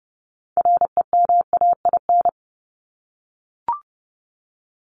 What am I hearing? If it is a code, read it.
Morse code, heard as a keyed tone, about 30 wpm, 700 Hz. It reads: REMAIN
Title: ES:S Suffix - Top 100 Words with Base Words 30wpm